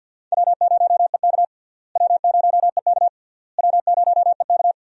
Morse code, heard as a keyed tone, about 50 wpm, 700 Hz. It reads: W0EK W0EK W0EK
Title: USA Call Signs - 4 Characters - Repeated 3x - 50wpm